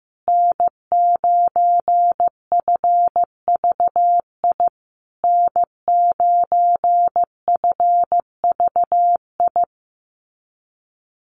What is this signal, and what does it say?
Morse code, heard as a keyed tone, about 15 wpm, 700 Hz. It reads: N9FVI N9FVI
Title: USA Call Signs Repeated 3x 15wpm